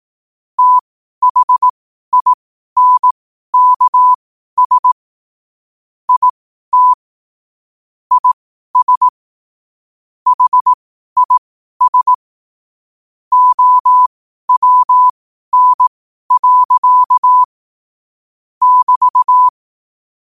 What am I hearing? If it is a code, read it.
Morse code, heard as a keyed tone, about 18 wpm, spaced as for 12 wpm, 1000 Hz. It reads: THINKS IT IS HIS OWN. <BT>
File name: qotdfiles / 12wpm